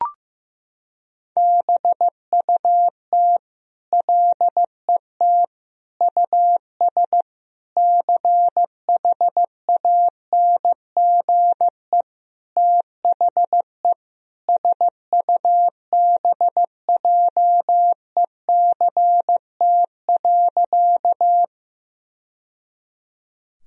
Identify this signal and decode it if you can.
Morse code, heard as a keyed tone, about 15 wpm, 700 Hz. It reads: BUT LET US CHANGE THE SUBJECT.
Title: Sentences from Top 1000 Words - Review Part 1 - 15wpm